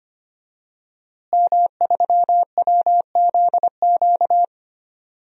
Morse code, heard as a keyed tone, about 25 wpm, 700 Hz. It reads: M3WZQ
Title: DXCC-100 Call Signs - 5 Characters - Repeated 2x - 25wpm